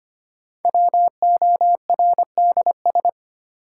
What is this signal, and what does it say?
Morse code, heard as a keyed tone, about 25 wpm, 700 Hz. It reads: WORDS
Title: Sentences from Top 1000 Words - Review Part 6 - New Word Drill - 25wpm